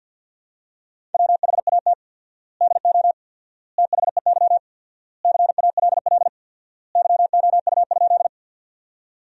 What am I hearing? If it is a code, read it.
Morse code, heard as a keyed tone, about 53 wpm, 700 Hz. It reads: WHAT DO THEY CALL YOU?
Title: 4x Word-Spacing - Sentences from Top 100 Words - 50wpm